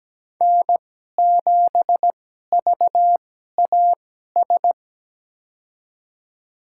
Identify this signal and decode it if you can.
Morse code, heard as a keyed tone, about 17 wpm, spaced as for 12 wpm, 700 Hz. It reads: N7VAS
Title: USA Call Signs - 2x Character Spacing - 17wpm